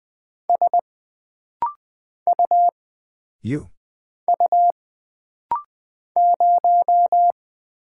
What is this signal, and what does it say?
Morse code, heard as a keyed tone, about 20 wpm, 700 Hz. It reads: S U U 0
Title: Single Character (Letter:Number:Punctuation:Prosign) - Rapid-Fire - 20wpm